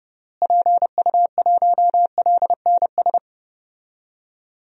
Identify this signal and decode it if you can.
Morse code, heard as a keyed tone, about 32 wpm, 700 Hz. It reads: PU1LNS
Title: DXCC-100 Call Signs - 6 Characters - Repeated 3x - 30wpm